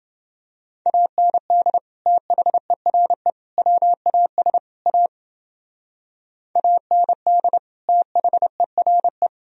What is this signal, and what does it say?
Morse code, heard as a keyed tone, about 30 wpm, 700 Hz. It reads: AND THERE WAS A AND THERE